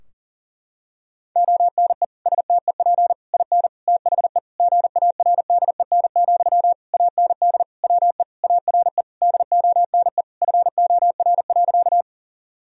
Morse code, heard as a keyed tone, about 40 wpm, 700 Hz. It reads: ONE STEP IN THE GARDEN, AND WE ARE DONE FOR.
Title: Sentences from Top 1000 Words - Review Part 4 - 40wpm